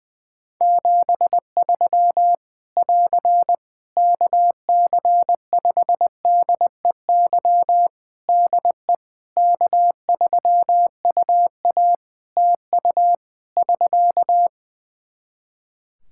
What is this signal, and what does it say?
Morse code, heard as a keyed tone, about 21 wpm, 700 Hz. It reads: 73 <AR> KC5DEY DE K3UA TU <SK>
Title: Practice QSO - Exchange 4 - Ending - 20wpm